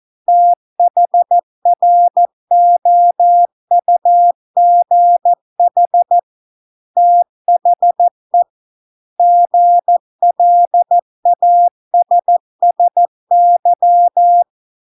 Morse code, heard as a keyed tone, about 14 wpm, 700 Hz. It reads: THROUGH THE GLASSY